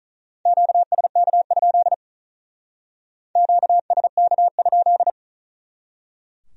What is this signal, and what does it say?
Morse code, heard as a keyed tone, about 43 wpm, 700 Hz. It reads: QSK? QSK?
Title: Speed-Racing Q-Codes - 35wpm